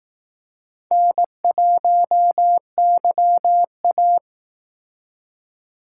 Morse code, heard as a keyed tone, about 18 wpm, 700 Hz. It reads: N1YA
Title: Speed Racing - USA 1x2 Call Signs 15wpm